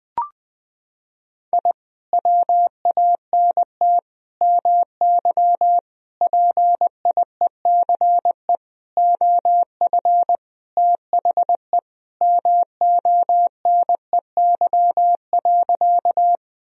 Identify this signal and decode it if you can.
Morse code, heard as a keyed tone, about 20 wpm, 700 Hz. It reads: I WANT MY PIECE OF THE MONEY.
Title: Sentences from Top 1000 Words - Review Part 2 - 20wpm